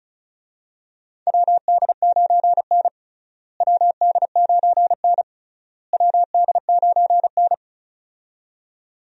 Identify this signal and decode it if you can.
Morse code, heard as a keyed tone, about 36 wpm, 700 Hz. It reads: WD9N WD9N WD9N
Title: USA Call Signs - 4 Characters - Repeated 3x - 35wpm